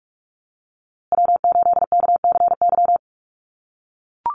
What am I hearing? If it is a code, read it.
Morse code, heard as a keyed tone, about 44 wpm, 700 Hz. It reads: W8KCY
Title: USA Call Signs Repeated 2x 45wpm